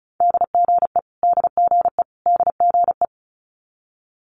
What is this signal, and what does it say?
Morse code, heard as a keyed tone, about 36 wpm, 700 Hz. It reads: DGE DGE DGE
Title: Multi-letter Phonemes Repeated 3x 35wpm